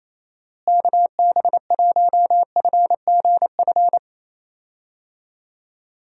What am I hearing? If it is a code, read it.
Morse code, heard as a keyed tone, about 27 wpm, 700 Hz. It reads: KB1FGF
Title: USA Call Signs - 6 Characters - Repeated 2x - 28wpm